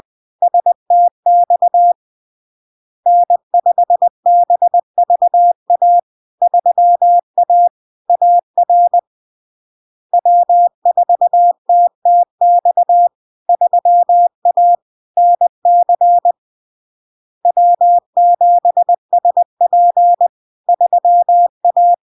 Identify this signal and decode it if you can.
Morse code, heard as a keyed tone, about 20 wpm, 700 Hz. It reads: STX N5BVA 3A AR W4TTX 3A NC W7SP 3A
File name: LARK-FD-2022-Log-in-CW-20wpm-700-Hz